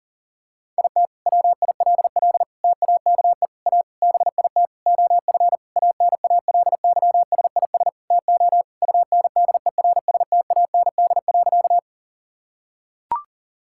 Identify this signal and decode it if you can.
Morse code, heard as a keyed tone, about 40 wpm, 700 Hz. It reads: IT WILL TAKE A BIT OF ANALYSIS TO UNDERSTAND.
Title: Sentences from Top 1000 Words - Review Part 5 - 40wpm